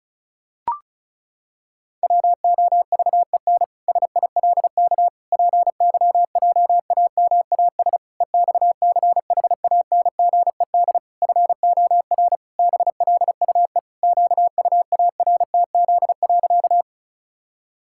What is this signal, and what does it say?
Morse code, heard as a keyed tone, about 36 wpm, 700 Hz. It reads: WOVEN SILK PYJAMAS EXCHANGED FOR BLUE QUARTZ.
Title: Pangram Challenge v2 - 35wpm